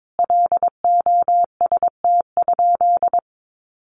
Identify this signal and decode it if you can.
Morse code, heard as a keyed tone, about 22 wpm, 700 Hz. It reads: LOST?
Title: Sentences from Top 1400 Words 22wpm